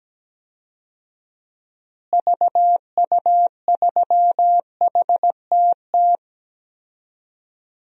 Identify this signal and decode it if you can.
Morse code, heard as a keyed tone, about 17 wpm, 700 Hz. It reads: VU3HTT
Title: DXCC-100 Call Signs - 6 Characters - Repeated 2x - 17wpm